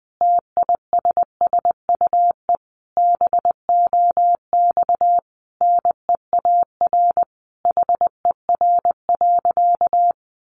Morse code, heard as a keyed tone, about 20 wpm, 700 Hz. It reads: TISSUE BOX NEAR HER.
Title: Easy Phrases 20wpm